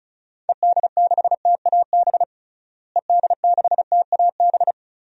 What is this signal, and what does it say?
Morse code, heard as a keyed tone, about 36 wpm, 700 Hz. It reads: ED6TAB ED6TAB
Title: DXCC-100 Call Signs - 6 Characters - Repeated 3x - 35wpm